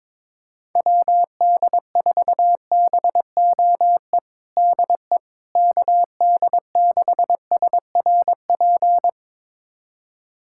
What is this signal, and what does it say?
Morse code, heard as a keyed tone, about 22 wpm, 700 Hz. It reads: WD4BOE DE KD6SRP
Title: USA Callsign DE Callsign 22wpm